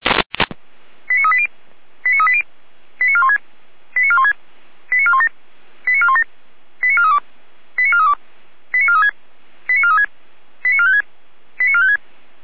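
Five-tone selective call (ZVEI), heard as quick runs of five tones, tones of about 70 ms. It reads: 89380 89380 89416 89416 89417 89417 89432 89432 89436 89436 89456 89456
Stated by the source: ZVEI five-tone call 89402 received by SDR.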